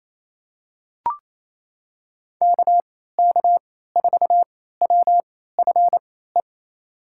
Morse code, heard as a keyed tone, about 28 wpm, spaced as for 16 wpm, 700 Hz. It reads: KK4WFE
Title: USA Call Signs - 6 Characters - 3x Character Spacing - 28wpm